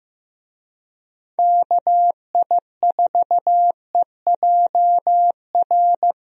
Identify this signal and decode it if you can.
Morse code, heard as a keyed tone, about 15 wpm, 700 Hz. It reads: KI4EJR
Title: USA Call Signs - 6 Characters - Repeated 3x - 15wpm